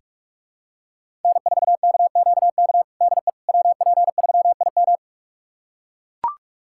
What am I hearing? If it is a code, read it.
Morse code, heard as a keyed tone, about 41 wpm, 700 Hz. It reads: N4KQK DE WP3IM
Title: USA Callsign DE Callsign - 5 Characters - 45wpm